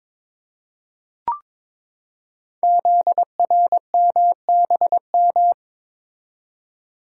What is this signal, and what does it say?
Morse code, heard as a keyed tone, about 22 wpm, 700 Hz. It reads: ZRMBM
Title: Code Groups - 5 Letters - 22wpm